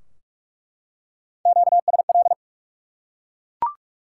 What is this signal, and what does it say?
Morse code, heard as a keyed tone, about 47 wpm, 700 Hz. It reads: QSL
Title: Top 180 QSO Elements 45wpm